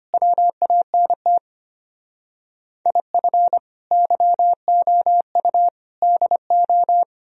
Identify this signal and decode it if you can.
Morse code, heard as a keyed tone, about 33 wpm, 700 Hz. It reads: WANT IF YOU DO